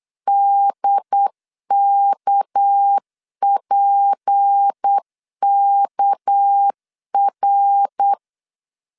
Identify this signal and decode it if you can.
Morse code, heard as a keyed tone, about 8 wpm, 800 Hz. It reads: DKPKR
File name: LEZIONI / 6 PKX0 / 6 PK 8wpm 3003